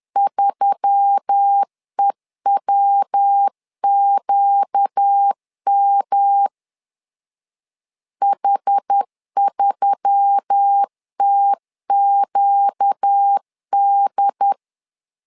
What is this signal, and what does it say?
Morse code, heard as a keyed tone, about 10 wpm, 800 Hz. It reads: 3EWQM H3TQD